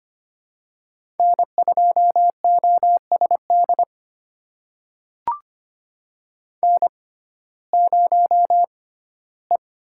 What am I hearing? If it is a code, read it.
Morse code, heard as a keyed tone, about 25 wpm, spaced as for 15 wpm, 700 Hz. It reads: N2OSD N0E
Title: USA Call Signs - 5 Characters - 6x Character Spacing - 25wpm